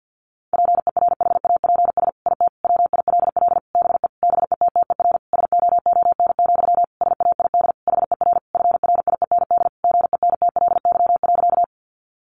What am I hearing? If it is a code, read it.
Morse code, heard as a keyed tone, about 50 wpm, 700 Hz. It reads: PERHAPS IT WILL BE BETTER SOON, SAID HER FRIEND GENTLY.
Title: Sentences from Top 1500 Words 50wpm